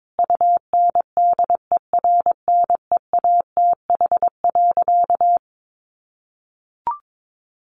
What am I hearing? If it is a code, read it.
Morse code, heard as a keyed tone, about 22 wpm, 700 Hz. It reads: UNDERNEATH.
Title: A Sound Of Thunder 22wpm